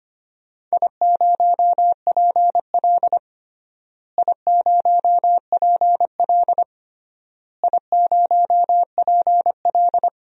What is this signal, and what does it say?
Morse code, heard as a keyed tone, about 25 wpm, 700 Hz. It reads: I0PL I0PL I0PL
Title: DXCC-100 Call Signs - 4 Characters - Repeated 3x - 25wpm